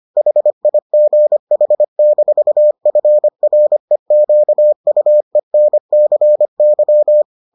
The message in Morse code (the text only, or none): HIGH-FREQUENCY